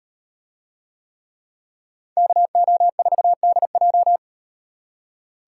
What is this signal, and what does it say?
Morse code, heard as a keyed tone, about 37 wpm, 700 Hz. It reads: KO4DJ